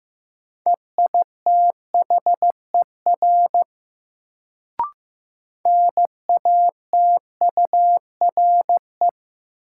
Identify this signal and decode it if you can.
Morse code, heard as a keyed tone, about 15 wpm, 700 Hz. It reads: EITHER NATURE
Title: Sentences from Top 600 Words - New Word Drill - 15wpm